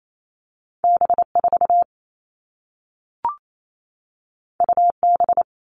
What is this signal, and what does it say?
Morse code, heard as a keyed tone, about 28 wpm, 700 Hz. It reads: B4 UB
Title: All Cut Numbers - 2-Digit 28wpm